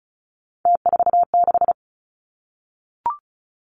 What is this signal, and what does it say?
Morse code, heard as a keyed tone, about 36 wpm, 700 Hz. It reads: T46
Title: Three Challenging Letter-Number Combinations 35wpm